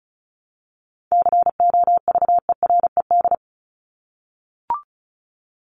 Morse code, heard as a keyed tone, about 34 wpm, 700 Hz. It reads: COVERED
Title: ED Suffix - Top 100 Words with Base Words 35wpm